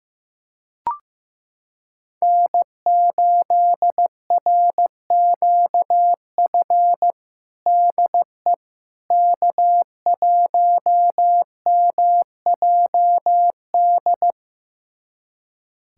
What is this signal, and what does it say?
Morse code, heard as a keyed tone, about 15 wpm, 700 Hz. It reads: N8RQF DE K1MJD
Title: USA Callsign DE Callsign - 5 Characters - 15wpm